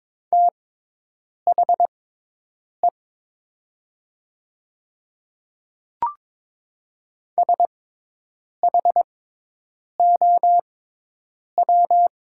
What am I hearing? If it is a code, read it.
Morse code, heard as a keyed tone, about 22 wpm, spaced as for 8 wpm, 700 Hz. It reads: THE SHOW
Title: 6x Character Spacing - Top 200 Words - 22wpm